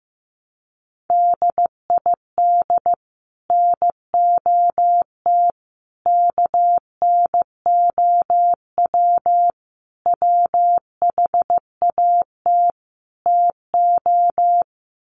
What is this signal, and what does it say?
Morse code, heard as a keyed tone, about 15 wpm, 700 Hz. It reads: DID NOT KNOW WHAT TO
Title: Sets of 5 Words 15wpm